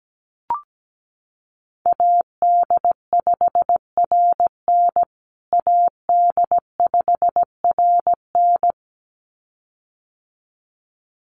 Morse code, heard as a keyed tone, about 17 wpm, 700 Hz. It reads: AD5RN AD5RN
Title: USA Call Signs Repeated 2x 17wpm